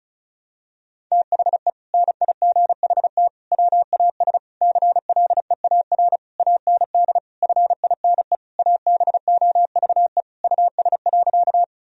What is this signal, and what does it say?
Morse code, heard as a keyed tone, about 34 wpm, 700 Hz. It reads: THE NIGHT WAS CLEAR AND FINE ABOVE US.
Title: Sentences from Top 1000 Words - Review Part 2 - 35wpm